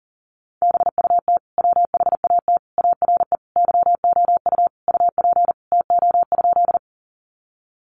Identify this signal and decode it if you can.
Morse code, heard as a keyed tone, about 37 wpm, 700 Hz. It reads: BUT WHAT ARE YOU UP TO?
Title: Sentences from Top 100 Words 40wpm